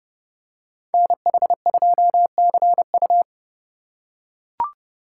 Morse code, heard as a keyed tone, about 30 wpm, 700 Hz. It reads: NH2CU